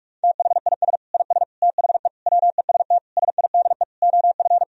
Morse code, heard as a keyed tone, about 46 wpm, 700 Hz. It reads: THIS IS THE WEST SIDE OF
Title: Sentences from Top 1000 Words - Review Part 2 - 45wpm